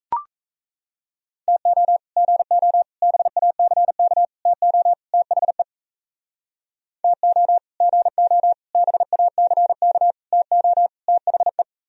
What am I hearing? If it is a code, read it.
Morse code, heard as a keyed tone, about 36 wpm, 700 Hz. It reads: TO GO BACK TO THE TO GO BACK TO THE